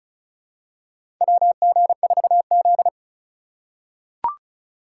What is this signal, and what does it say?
Morse code, heard as a keyed tone, about 36 wpm, 700 Hz. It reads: WG4Z
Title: USA Call Signs - 4x Character Spacing - 35wpm